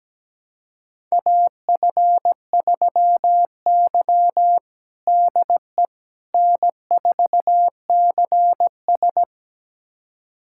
Morse code, heard as a keyed tone, about 17 wpm, 700 Hz. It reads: AF3Y DE N4CS
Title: USA Callsign DE Callsign - 4 Characters - 17wpm